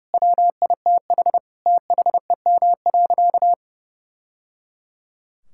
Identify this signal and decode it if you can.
Morse code, heard as a keyed tone, about 30 wpm, 700 Hz. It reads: WITH THEM.
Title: Sentences from Top 1000 Words - Review Part 6 - 30wpm